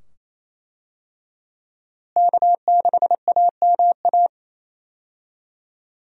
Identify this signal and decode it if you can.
Morse code, heard as a keyed tone, about 27 wpm, 700 Hz. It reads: K6AMA
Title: USA Call Signs - 5 Characters - Repeated 2x - 28wpm